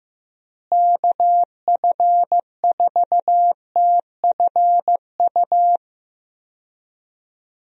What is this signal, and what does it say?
Morse code, heard as a keyed tone, about 15 wpm, 700 Hz. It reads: KF4TFU
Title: USA Call Signs - 6 Characters - Repeated 3x - 15wpm